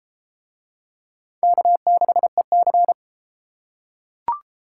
Morse code, heard as a keyed tone, about 32 wpm, 700 Hz. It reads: K6EC